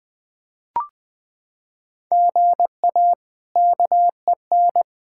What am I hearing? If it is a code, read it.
Morse code, heard as a keyed tone, about 21 wpm, 700 Hz. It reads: GA KEN
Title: K1USN SST QSO Practice - 20wpm